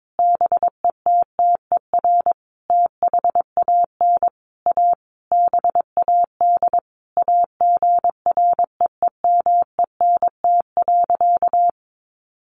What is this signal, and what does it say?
Morse code, heard as a keyed tone, about 22 wpm, 700 Hz. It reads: BETTER THAN A BAD AGREEMENT.
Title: Sentences from Top 1500 Words 22wpm